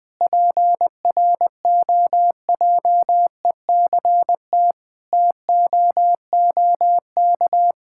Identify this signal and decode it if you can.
Morse code, heard as a keyed tone, about 20 wpm, 700 Hz. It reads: PROJECT TOOK
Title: Sentences with 3-Word Phrases - 20wpm